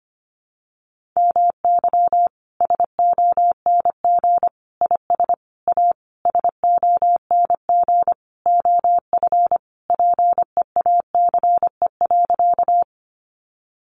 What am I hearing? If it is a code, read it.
Morse code, heard as a keyed tone, about 25 wpm, 700 Hz. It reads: MY SONG IS A SONG OF PEACE.
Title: Sentences from Top 1500 Words 25wpm